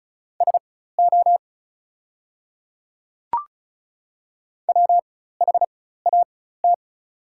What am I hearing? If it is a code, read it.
Morse code, heard as a keyed tone, about 36 wpm, spaced as for 16 wpm, 690 Hz. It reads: SO WHAT
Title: 4x Character Spacing - Top 300 Words - 35wpm